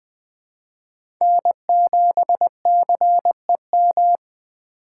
Morse code, heard as a keyed tone, about 20 wpm, 700 Hz. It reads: N7CEM
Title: USA Call Signs - 5 Characters - 6x Character Spacing - 20wpm